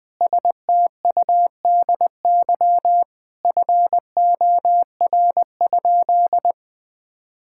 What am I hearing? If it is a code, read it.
Morse code, heard as a keyed tone, about 20 wpm, 700 Hz. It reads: STUDY FOR?
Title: Sentences from Top 1000 Words - Review Part 6 - 20wpm